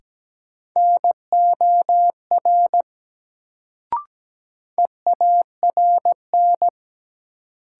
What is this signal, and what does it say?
Morse code, heard as a keyed tone, about 17 wpm, 700 Hz. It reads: NOR EARN
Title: Sentences from Top 1500 Words - New Word Drill Alt - 17wpm